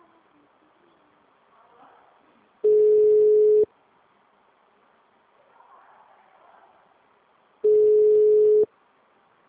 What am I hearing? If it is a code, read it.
Ringback tone (Europe, 425 Hz).
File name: out- -2016- - - .28